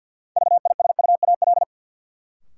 Morse code, heard as a keyed tone, about 52 wpm, 700 Hz. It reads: VISUAL
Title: Sets of 5 Words Encore - New Word Drill - 50wpm